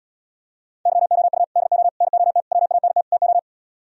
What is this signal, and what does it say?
Morse code, heard as a keyed tone, about 77 wpm, 700 Hz. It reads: YOU DO NOT WANT TO